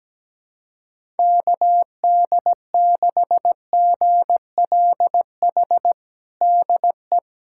Morse code, heard as a keyed tone, about 17 wpm, 700 Hz. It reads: KD6GLH DE
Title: USA Callsign DE Callsign - 6 Characters - 17wpm